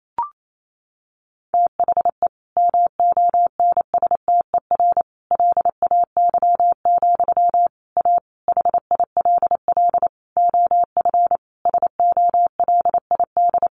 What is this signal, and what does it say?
Morse code, heard as a keyed tone, about 28 wpm, 700 Hz. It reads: THE MONSTER LAY, A HILL OF SOLID
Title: A Sound Of Thunder 28wpm